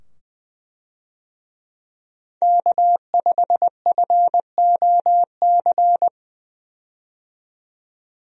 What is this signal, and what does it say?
Morse code, heard as a keyed tone, about 20 wpm, 700 Hz. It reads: K5FOC
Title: USA Call Signs - 5 Characters - 20wpm